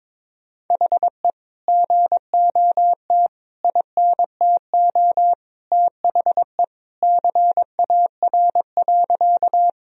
Morse code, heard as a keyed tone, about 22 wpm, 700 Hz. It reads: HE GOT INTO THE CAR.
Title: Sentences from Top 1000 Words - Review Part 1 - 22wpm